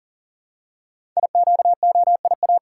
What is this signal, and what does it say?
Morse code, heard as a keyed tone, about 43 wpm, 700 Hz. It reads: IQOIA
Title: Code Groups - 5 Letters - 40wpm